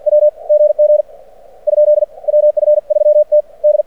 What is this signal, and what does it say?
Morse code, heard as a keyed tone, about 48 wpm, spaced as for 40 wpm, 600 Hz. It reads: WGG ? WF3T D
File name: archived / CRSnet